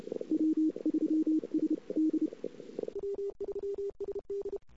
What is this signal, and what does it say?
Morse code, heard as a keyed tone, about 29 wpm, spaced as for 24 wpm, 310 Hz. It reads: W3SD
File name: CW / 2008-ss-cw-k5tr-op-n5rz -21:00